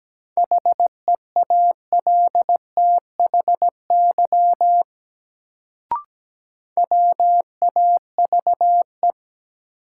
Morse code, heard as a keyed tone, about 17 wpm, 700 Hz. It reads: HEALTHY WAVE
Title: Sentences from Top 1400 Words - New Word Drill Alt - 17wpm